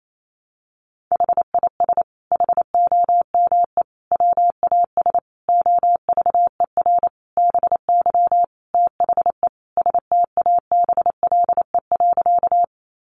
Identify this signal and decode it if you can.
Morse code, heard as a keyed tone, about 28 wpm, 700 Hz. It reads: HIS HOME WAS OVER BY THE STABLE.
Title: Sentences from Top 1500 Words 28wpm